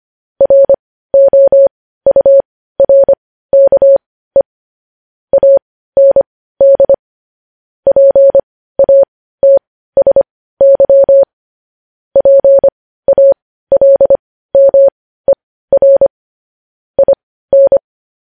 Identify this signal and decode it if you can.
Morse code, heard as a keyed tone, about 25 wpm, spaced as for 15 wpm, 550 Hz. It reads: ROURKE AND PATSY PALMER IN
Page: news-headlines-in-morse-code-at-15-wpm